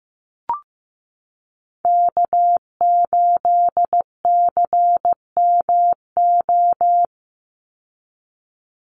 Morse code, heard as a keyed tone, about 15 wpm, 700 Hz. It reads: K8CMO
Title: Call Signs Repeated 3x - 15wpm